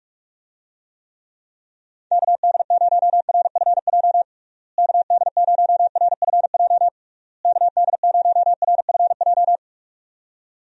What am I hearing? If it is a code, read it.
Morse code, heard as a keyed tone, about 42 wpm, 700 Hz. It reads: KD0RFJ KD0RFJ KD0RFJ
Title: USA Call Signs - 6 Characters - Repeated 3x - 45wpm